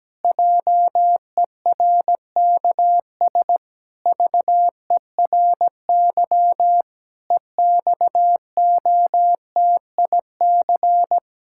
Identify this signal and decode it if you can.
Morse code, heard as a keyed tone, about 17 wpm, 700 Hz. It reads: JERKS VERY EXOTIC
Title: Pangram Challenge v2 - 17wpm